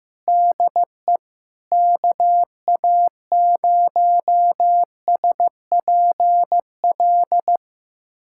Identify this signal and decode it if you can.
Morse code, heard as a keyed tone, about 15 wpm, 700 Hz. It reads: DE KA0SPL
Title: USA Callsign DE Callsign - 6 Characters - 15wpm